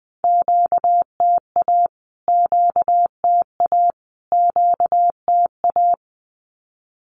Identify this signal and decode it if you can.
Morse code, heard as a keyed tone, about 20 wpm, 700 Hz. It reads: QTA QTA QTA
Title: Q-Codes Repeated 3x 20wpm